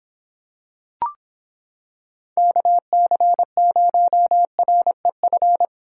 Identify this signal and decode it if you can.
Morse code, heard as a keyed tone, about 26 wpm, 700 Hz. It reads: KC0REF